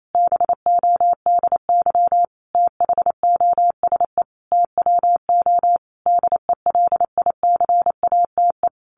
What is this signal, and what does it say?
Morse code, heard as a keyed tone, about 28 wpm, 700 Hz. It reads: BODY THOSE TWO DELICATE
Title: A Sound Of Thunder 28wpm